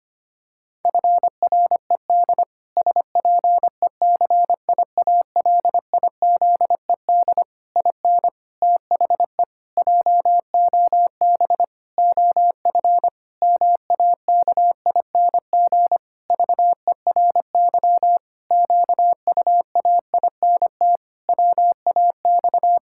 Morse code, heard as a keyed tone, about 25 wpm, 700 Hz. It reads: FRED SPECIALIZED IN THE JOB OF MAKING VERY QUAINT WAX
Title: Pangram Challenge v2 - 25wpm